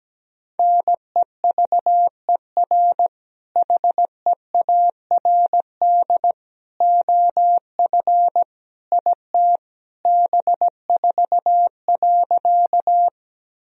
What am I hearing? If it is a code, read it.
Morse code, heard as a keyed tone, about 17 wpm, 700 Hz. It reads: NEVER HEARD OF IT B4.
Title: Sentences from Top 400 Words and Abbreviations 17wpm